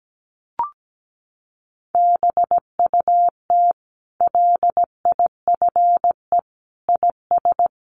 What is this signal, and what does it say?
Morse code, heard as a keyed tone, about 17 wpm, 700 Hz. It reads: BUT LIFE IS
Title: Sentences from Top 900 Words 17wpm